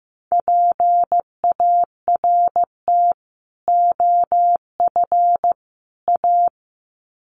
Morse code, heard as a keyed tone, about 15 wpm, 700 Hz. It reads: PART OF A
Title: Sets of 3 Words 15wpm